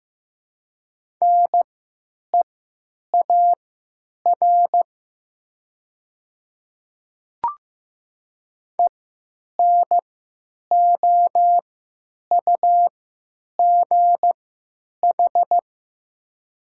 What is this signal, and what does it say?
Morse code, heard as a keyed tone, about 15 wpm, spaced as for 9 wpm, 700 Hz. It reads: NEAR ENOUGH
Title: 3x Character Spacing - Top 500 Words - 15wpm